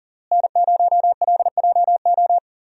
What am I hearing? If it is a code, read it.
Morse code, heard as a keyed tone, about 37 wpm, 700 Hz. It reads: N0LJO
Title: USA Call Signs - 5 Characters - Repeated 3x - 40wpm